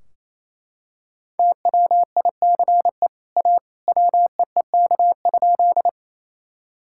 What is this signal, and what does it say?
Morse code, heard as a keyed tone, about 28 wpm, 700 Hz. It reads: TWICE A WEEK?
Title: Sentences from Top 1600 Words 28wpm